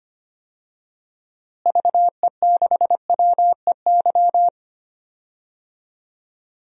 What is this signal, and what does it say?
Morse code, heard as a keyed tone, about 25 wpm, 700 Hz. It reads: VE6WEY
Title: DXCC-100 Call Signs - 6 Characters - Repeated 2x - 25wpm